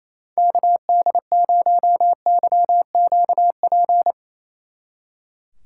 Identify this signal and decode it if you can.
Morse code, heard as a keyed tone, about 35 wpm, 700 Hz. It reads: KD0YQP
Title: Speed Racing - USA 2x3 Call Signs 28wpm